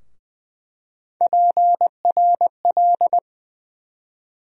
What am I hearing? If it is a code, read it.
Morse code, heard as a keyed tone, about 20 wpm, 700 Hz. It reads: PRL
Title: USA Call Sign Postfixes - 3 Letters 20wpm